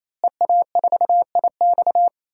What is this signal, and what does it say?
Morse code, heard as a keyed tone, about 28 wpm, 700 Hz. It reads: EA4IX
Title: DXCC-100 Call Signs - 5 Characters - Repeated 3x - 28wpm